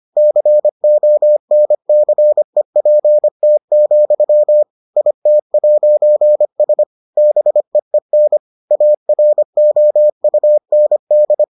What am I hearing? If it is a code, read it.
Morse code, heard as a keyed tone, about 25 wpm, 600 Hz. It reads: CONCEPT, IT'S BEEN AROUND